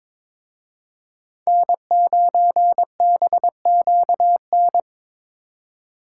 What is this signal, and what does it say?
Morse code, heard as a keyed tone, about 26 wpm, 700 Hz. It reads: N9BQN
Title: Speed Racing - USA 1x3 Call Signs 22wpm